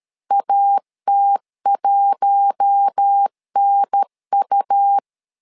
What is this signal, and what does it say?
Morse code, heard as a keyed tone, about 13 wpm, 800 Hz. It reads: AT1NU
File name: LEZIONI / 4 Q1 12wpm 2028